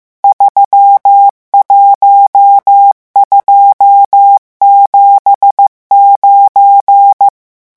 Morse code, heard as a keyed tone, about 15 wpm, 800 Hz. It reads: 31279
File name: morse mp3 / revision11 chiffres vitesse 14